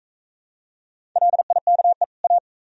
Morse code, heard as a keyed tone, about 39 wpm, 700 Hz. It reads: LIKE A